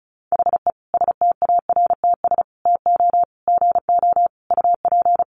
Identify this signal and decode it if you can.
Morse code, heard as a keyed tone, about 34 wpm, 700 Hz. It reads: HE STARTS TO GO UP
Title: Additional Easy Phrases 35wpm